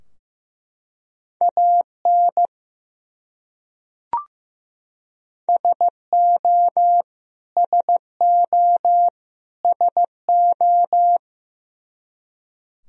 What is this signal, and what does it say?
Morse code, heard as a keyed tone, about 15 wpm, 700 Hz. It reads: AN SO SO SO
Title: Top 100 Words Repeated 15wpm